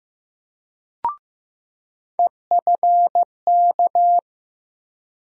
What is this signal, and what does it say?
Morse code, heard as a keyed tone, about 16 wpm, 700 Hz. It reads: EFK
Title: USA Call Sign Postfixes - 3 Letters 15wpm